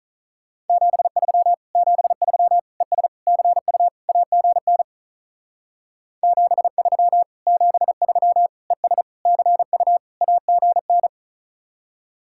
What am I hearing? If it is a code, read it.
Morse code, heard as a keyed tone, about 45 wpm, 700 Hz. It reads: 73 73 ES CU AGN 73 73 ES CU AGN